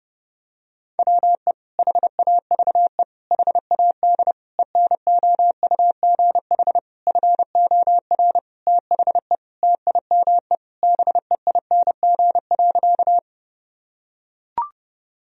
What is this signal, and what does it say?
Morse code, heard as a keyed tone, about 32 wpm, 700 Hz. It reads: WE HAVE HAD ENOUGH FOR THE TIME BEING.
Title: Sentences from Top 1000 Words - Review Part 3 - 30wpm